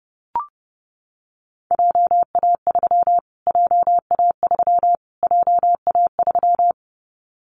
Morse code, heard as a keyed tone, about 30 wpm, 700 Hz. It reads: JA3 JA3 JA3
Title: Callsign Prefixes for Top 25 DXCC - Repeated 3x - 30wpm